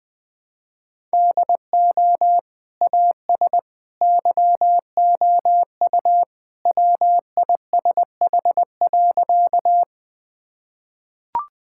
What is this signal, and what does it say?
Morse code, heard as a keyed tone, about 21 wpm, 700 Hz. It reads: DO AS YOU WISH.
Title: Sentences with 2-Word Phrases - 20wpm